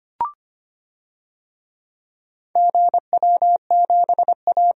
Morse code, heard as a keyed tone, about 25 wpm, 700 Hz. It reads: GW7A
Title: DXCC-100 Call Signs - 4 Characters - 25wpm